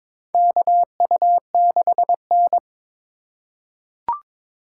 Morse code, heard as a keyed tone, about 22 wpm, 700 Hz. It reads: KU6N
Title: USA Call Signs - 4 Characters - 3x Character Spacing - 22wpm